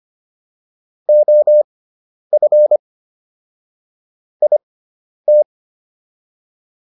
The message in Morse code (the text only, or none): OF IT